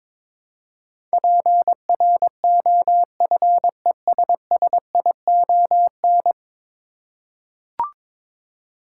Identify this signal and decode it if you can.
Morse code, heard as a keyed tone, about 22 wpm, 700 Hz. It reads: PROFESSION
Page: morse-code-ninja-practice-22wpm